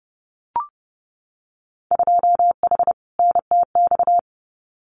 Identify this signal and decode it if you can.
Morse code, heard as a keyed tone, about 30 wpm, 700 Hz. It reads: 2H NTX
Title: Winter Field Day Exchanges Encore 30wpm